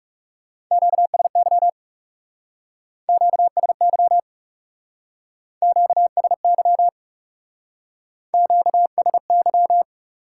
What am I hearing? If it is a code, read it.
Morse code, heard as a keyed tone, about 36 wpm, 700 Hz. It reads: QSY QSY QSY QSY
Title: Speed-Racing Top 180 QSO Elements 30wpm